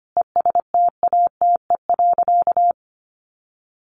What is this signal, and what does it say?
Morse code, heard as a keyed tone, about 25 wpm, 700 Hz. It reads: ESTATE.
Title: Sentences from Top 1200 Words 25wpm